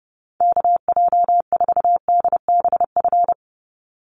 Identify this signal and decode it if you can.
Morse code, heard as a keyed tone, about 30 wpm, 700 Hz. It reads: KJ4DBF
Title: USA Call Signs 30wpm